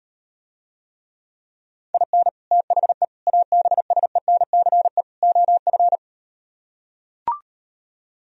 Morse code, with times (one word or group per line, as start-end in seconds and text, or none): 1.9-2.3 IN
2.5-3.0 THE
3.3-5.0 ABSENCE
5.2-6.0 OF